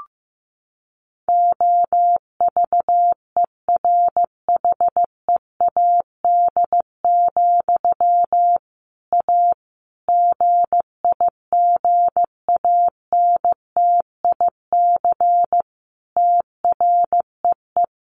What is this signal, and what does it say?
Morse code, heard as a keyed tone, about 15 wpm, 700 Hz. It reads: OVERHEAD, A GIGANTIC TREE
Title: A Sound Of Thunder 15wpm